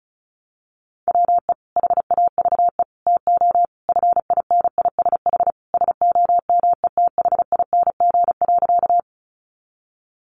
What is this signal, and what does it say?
Morse code, heard as a keyed tone, about 34 wpm, 700 Hz. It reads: WE HAVE TO FINISH SOMETHING.
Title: Sentences from Top 1200 Words 35wpm